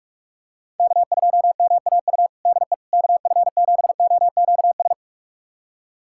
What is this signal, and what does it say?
Morse code, heard as a keyed tone, about 42 wpm, 690 Hz. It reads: K2MAU DE KF7OQS
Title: USA Callsign DE Callsign 45wpm